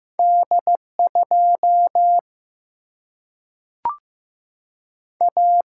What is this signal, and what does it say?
Morse code, heard as a keyed tone, about 15 wpm, 700 Hz. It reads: D2 A
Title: Single-Letter Single-Number 15wpm